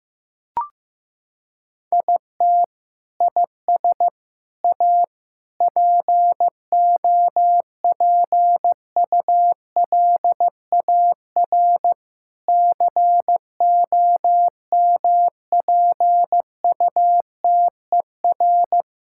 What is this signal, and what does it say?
Morse code, heard as a keyed tone, about 15 wpm, 700 Hz. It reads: IT IS A POPULAR COMPUTER
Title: Sentences from Top 1000 Words - Review Part 6 - 15wpm